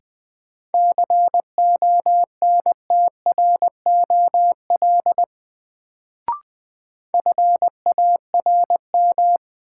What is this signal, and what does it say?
Morse code, heard as a keyed tone, about 21 wpm, 700 Hz. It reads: CONTROL FARM
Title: Sentences from Top 1000 Words - Review Part 2 - New Word Drill - 20wpm